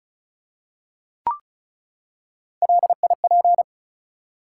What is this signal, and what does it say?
Morse code, heard as a keyed tone, about 36 wpm, 700 Hz. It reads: LIP
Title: Three Letter Words - 35wpm